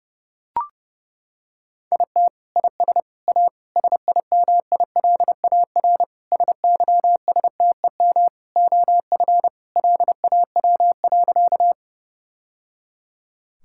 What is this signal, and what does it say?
Morse code, heard as a keyed tone, about 29 wpm, 700 Hz. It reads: IT IS A SIMILAR SYSTEM OF LAW.
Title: Sentences from Top 1000 Words - Review Part 5 - 30wpm